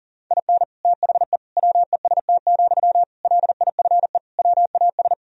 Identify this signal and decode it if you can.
Morse code, heard as a keyed tone, about 40 wpm, 700 Hz. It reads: IN THE WEST, LIFE WAS
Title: Sentences from Top 1000 Words - Review Part 2 - 40wpm